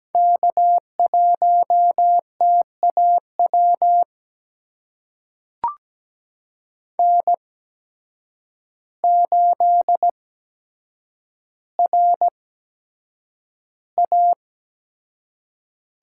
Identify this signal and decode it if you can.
Morse code, heard as a keyed tone, about 17 wpm, spaced as for 8 wpm, 700 Hz. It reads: K1TAW N8RA
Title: USA Call Signs - 5 Characters - 8x Character Spacing - 17wpm